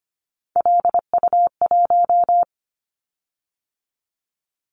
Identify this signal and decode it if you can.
Morse code, heard as a keyed tone, about 25 wpm, 700 Hz. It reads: LU1
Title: Callsign Prefixes for Top 100 DXCC - 25wpm